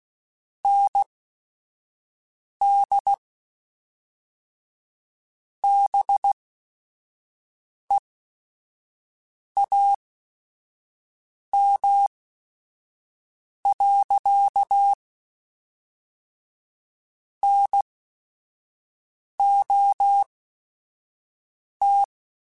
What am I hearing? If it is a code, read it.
Morse code, heard as a keyed tone, about 16 wpm, spaced as for 5 wpm, 800 Hz. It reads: ND BEAM. NOT
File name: QSOs / QSOs-mp3s / WD0BGP 1@5